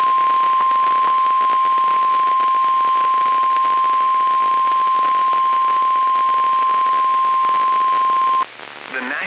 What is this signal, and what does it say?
Emergency Alert System (SAME) tones (the 1050 Hz attention tone), with speech. It is heard from 1.1 to 8.5 s.
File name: eas messages / EAS WXL38 Binghamton